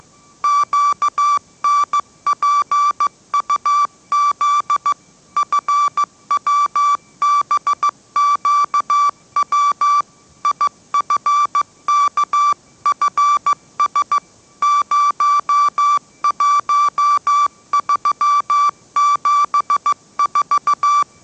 Morse code, heard as a keyed tone, about 15 wpm, 1160 Hz. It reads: QNPUZ FWBQW IFKFS 01374
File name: mrx / mrx065 16wpm